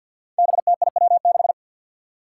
Morse code, heard as a keyed tone, about 52 wpm, 700 Hz. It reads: BTIW6
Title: Five Challenging Letter-Number Combinations 50wpm